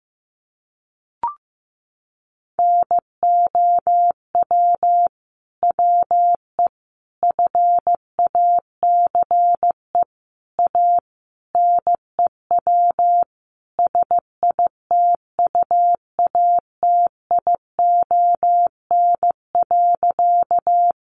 Morse code, heard as a keyed tone, about 15 wpm, 700 Hz. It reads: NOW WE FACE A NEW SITUATION.
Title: Sentences from Top 1200 Words 15wpm